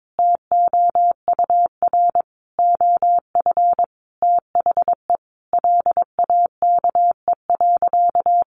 Morse code, heard as a keyed tone, about 22 wpm, 700 Hz. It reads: TOUR OF THE LAKE.
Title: Sentences from Top 1400 Words 22wpm